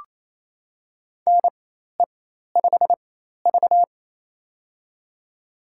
Morse code, heard as a keyed tone, about 28 wpm, spaced as for 13 wpm, 700 Hz. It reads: NE5V
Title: USA Call Signs - 4 Characters - 4x Character Spacing - 28wpm